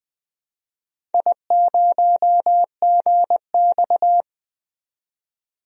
Morse code, heard as a keyed tone, about 20 wpm, 700 Hz. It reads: I0GX
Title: DXCC-100 Call Signs - 4 Characters - 20wpm